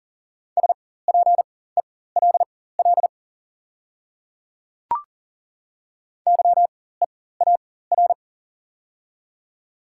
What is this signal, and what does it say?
Morse code, heard as a keyed tone, about 40 wpm, spaced as for 19 wpm, 690 Hz. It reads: SPELL YEAR
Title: 4x Character Spacing - Top 300 Words - 40wpm